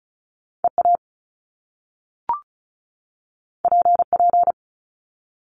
Morse code, heard as a keyed tone, about 36 wpm, 700 Hz. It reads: EA PP
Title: Top Two Letter Combinations in Words 35wpm